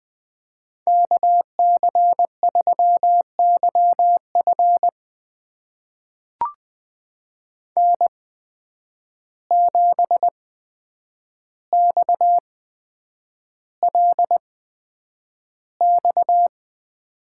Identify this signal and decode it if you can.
Morse code, heard as a keyed tone, about 20 wpm, spaced as for 9 wpm, 700 Hz. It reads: KC3YF N7XLX
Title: USA Call Signs - 5 Characters - 8x Character Spacing - 20wpm